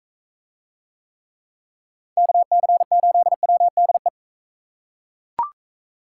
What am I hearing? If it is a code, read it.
Morse code, heard as a keyed tone, about 41 wpm, 700 Hz. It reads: KC8WDE